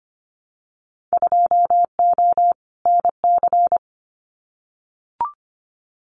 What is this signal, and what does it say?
Morse code, heard as a keyed tone, about 25 wpm, 690 Hz. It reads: 2O NC
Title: Winter Field Day Exchanges 25wpm